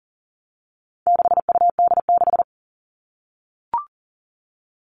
Morse code, heard as a keyed tone, about 40 wpm, 700 Hz. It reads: 6UD6
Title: All Cut Numbers - 4-Digit 40wpm